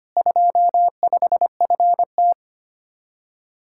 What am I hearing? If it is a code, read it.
Morse code, heard as a keyed tone, about 29 wpm, 700 Hz. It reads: 25FT
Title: Speed-Racing Common QSO Phrases - 25wpm